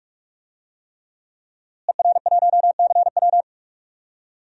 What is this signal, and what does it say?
Morse code, heard as a keyed tone, about 46 wpm, 700 Hz. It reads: ER1CW
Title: DXCC-100 Call Signs - 5 Characters - Repeated 3x - 45wpm